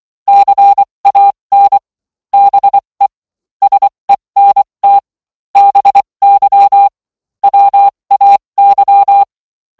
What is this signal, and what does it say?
Morse code, heard as a keyed tone, about 24 wpm, spaced as for 20 wpm, 800 Hz. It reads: CAN BE SENT BY WAY